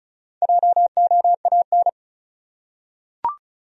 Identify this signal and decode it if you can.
Morse code, heard as a keyed tone, about 36 wpm, 700 Hz. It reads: JOAN
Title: USA Common Names - 35wpm